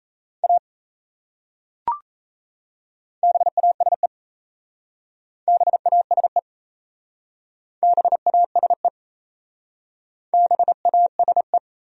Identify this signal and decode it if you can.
Morse code, heard as a keyed tone, about 37 wpm, 700 Hz. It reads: A BASE BASE BASE BASE